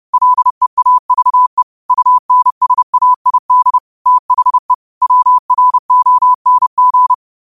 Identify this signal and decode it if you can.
Morse code, heard as a keyed tone, about 30 wpm, 1000 Hz. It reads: LEAVE UNSAID THE WRONG